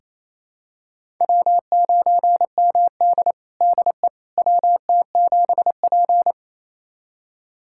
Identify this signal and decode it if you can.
Morse code, heard as a keyed tone, about 28 wpm, 700 Hz. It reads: W9MD DE WT7P
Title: USA Callsign DE Callsign - 4 Characters - 28wpm